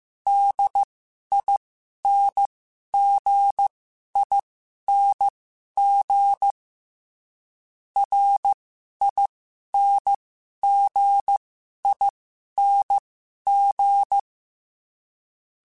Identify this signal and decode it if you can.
Morse code, heard as a keyed tone, about 15 wpm, spaced as for 11 wpm, 800 Hz. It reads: DINGING RINGING
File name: Special INGs / Special INGs-mp3s / Dble ings @10